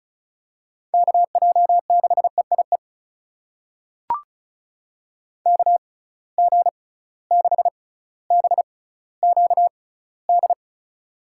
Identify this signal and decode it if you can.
Morse code, heard as a keyed tone, about 34 wpm, spaced as for 18 wpm, 700 Hz. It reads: KJ6EIE KG6BQD
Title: USA Call Signs - 6x Character Spacing - 35wpm